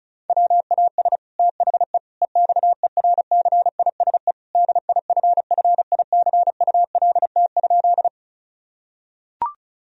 Morse code, heard as a keyed tone, about 36 wpm, 700 Hz. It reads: WAS THE EXERCISE DIFFICULT?
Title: Sentences from Top 1000 Words - Review Part 4 - 35wpm